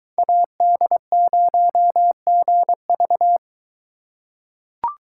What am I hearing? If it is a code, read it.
Morse code, heard as a keyed tone, about 21 wpm, 700 Hz. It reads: AD0GV